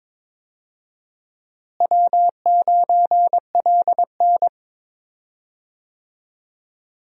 Morse code, heard as a keyed tone, about 22 wpm, 700 Hz. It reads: W9LN
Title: USA Call Signs - 4 Characters - Repeated 3x - 22wpm